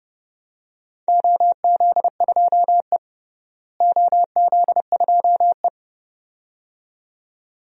Morse code, heard as a keyed tone, about 30 wpm, 700 Hz. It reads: OZ2E OZ2E
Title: DXCC-100 Call Signs - 4 Characters - Repeated 2x - 30wpm